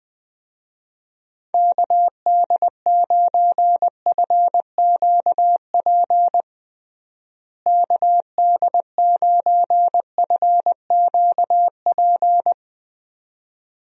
Morse code, heard as a keyed tone, about 21 wpm, 700 Hz. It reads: KD9FQP KD9FQP
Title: USA Call Signs - 6 Characters - Repeated 2x - 20wpm